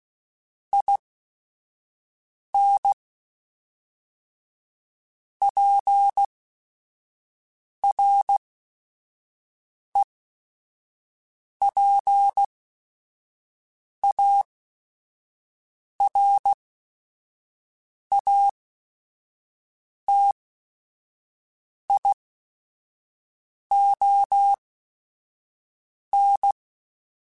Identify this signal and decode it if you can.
Morse code, heard as a keyed tone, about 16 wpm, spaced as for 5 wpm, 800 Hz. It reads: IN PREPARATION
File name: Quotes / Quotes-mp3s / Persperation @5